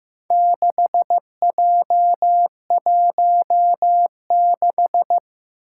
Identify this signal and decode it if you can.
Morse code, heard as a keyed tone, about 15 wpm, 700 Hz. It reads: 6J16
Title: Four Challenging Letter-Number Combinations 15wpm